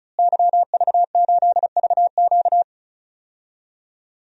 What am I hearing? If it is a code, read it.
Morse code, heard as a keyed tone, about 36 wpm, 700 Hz. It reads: YV8VQ
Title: DXCC-100 Call Signs - 5 Characters - Repeated 2x - 35wpm